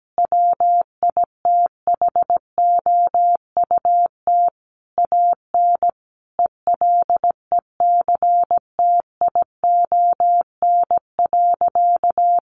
Morse code, heard as a keyed tone, about 17 wpm, 700 Hz. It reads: WITHOUT AN ELECTION.
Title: Sentences from Top 1500 Words 17wpm